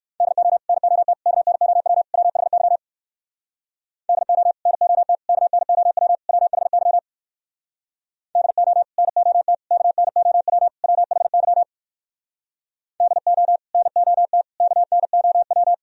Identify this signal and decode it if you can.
Morse code, heard as a keyed tone, about 59 wpm, 700 Hz. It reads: DO NOT KNOW WHY DO NOT KNOW WHY DO NOT KNOW WHY DO NOT KNOW